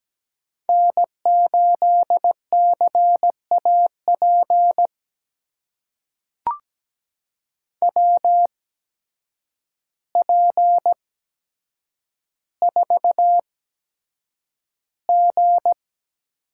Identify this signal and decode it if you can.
Morse code, heard as a keyed tone, about 17 wpm, spaced as for 8 wpm, 700 Hz. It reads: N8CAP WP4G
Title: USA Call Signs - 5 Characters - 8x Character Spacing - 17wpm